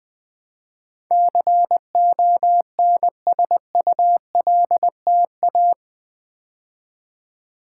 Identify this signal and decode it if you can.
Morse code, heard as a keyed tone, about 20 wpm, 700 Hz. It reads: CONSULTA
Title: Build A Word - 4 Letters At A Time - 12-Letter Words - 20wpm